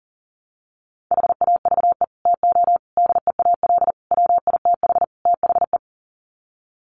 Morse code, heard as a keyed tone, about 40 wpm, 700 Hz. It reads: HAVE TO DEAL WITH THE
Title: Sets of 5 Words Encore2 40wpm